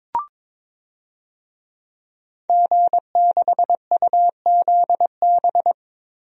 Morse code, heard as a keyed tone, about 22 wpm, 700 Hz. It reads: G6UZB
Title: DXCC-100 Call Signs - 5 Characters - 22wpm